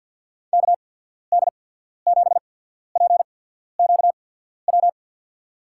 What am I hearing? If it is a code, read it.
Morse code, heard as a keyed tone, about 52 wpm, spaced as for 14 wpm, 700 Hz. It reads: KD7PQW
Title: USA Call Signs - 6 Characters - 8x Character Spacing - 50wpm